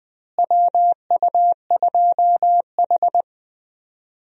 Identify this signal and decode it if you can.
Morse code, heard as a keyed tone, about 20 wpm, 700 Hz. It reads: WU2H
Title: USA Call Signs - 4 Characters - Repeated 3x - 20wpm